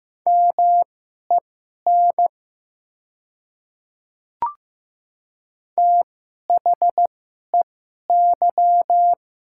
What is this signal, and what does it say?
Morse code, heard as a keyed tone, about 15 wpm, spaced as for 11 wpm, 700 Hz. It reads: MEN THEY
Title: 2x Character Spacing - Top 500 Words - 15wpm